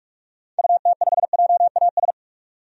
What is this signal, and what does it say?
Morse code, heard as a keyed tone, about 43 wpm, 700 Hz. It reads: UT5JAS
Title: DXCC-100 Call Signs - 6 Characters - 45wpm